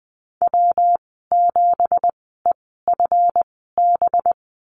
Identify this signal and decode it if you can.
Morse code, heard as a keyed tone, about 20 wpm, spaced as for 15 wpm, 700 Hz. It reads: W7EFB
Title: 186 Lesson 30g - Letter b - Call Signs 20wpm